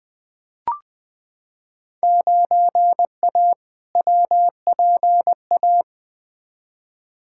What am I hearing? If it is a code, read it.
Morse code, heard as a keyed tone, about 20 wpm, 700 Hz. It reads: 9A WPA
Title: Field Day Exchanges Encore - 20wpm